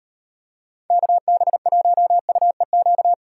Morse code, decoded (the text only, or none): KB1UEQ